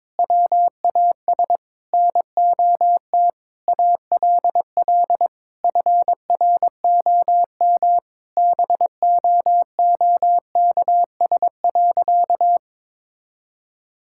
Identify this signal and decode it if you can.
Morse code, heard as a keyed tone, about 22 wpm, 700 Hz. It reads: WAS NOT ALL FROM BOOKS.
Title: Sentences from Top 1000 Words - Review Part 6 - 22wpm